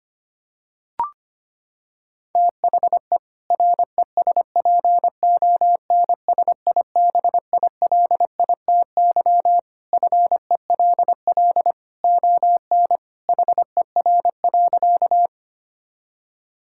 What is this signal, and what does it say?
Morse code, heard as a keyed tone, about 25 wpm, 700 Hz. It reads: THE RESPONSIBILITY FELL ON HER.
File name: practice / 25wpm / Sentences with 2-Word Phrases - 25wpm